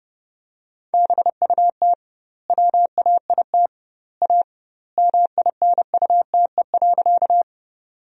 Morse code, heard as a keyed tone, about 30 wpm, 700 Hz. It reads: BUT WAIT A MINUTE.
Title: 2x Word-Spacing - Sentences from Top 500 Words - 30wpm